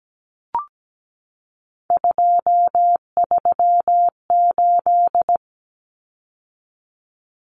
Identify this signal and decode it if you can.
Morse code, heard as a keyed tone, about 17 wpm, 700 Hz. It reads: 238
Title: Numbers - 3-Digit 17wpm